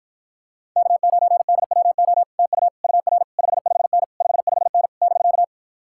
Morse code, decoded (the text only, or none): K9DWO TU UR 55N 55N <BK>